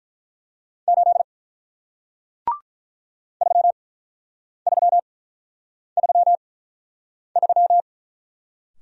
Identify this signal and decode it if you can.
Morse code, heard as a keyed tone, about 48 wpm, spaced as for 9 wpm, 700 Hz. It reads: 8 3333